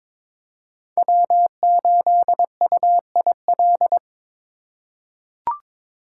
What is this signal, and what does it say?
Morse code, heard as a keyed tone, about 22 wpm, 700 Hz. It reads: W8UIL
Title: K1USN SST QSO Practice - 22wpm